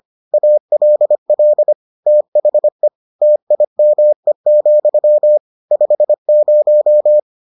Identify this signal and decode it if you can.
Morse code, heard as a keyed tone, about 25 wpm, 600 Hz. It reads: ALL THE TIME, 50